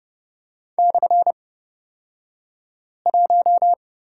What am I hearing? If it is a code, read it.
Morse code, heard as a keyed tone, about 30 wpm, spaced as for 5 wpm, 700 Hz. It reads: /1
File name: Lesson 37d - Letter x - Two Characters 30wpm